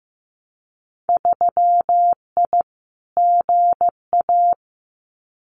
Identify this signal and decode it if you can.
Morse code, heard as a keyed tone, about 15 wpm, 700 Hz. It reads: 3I GA
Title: Winter Field Day Exchanges 15wpm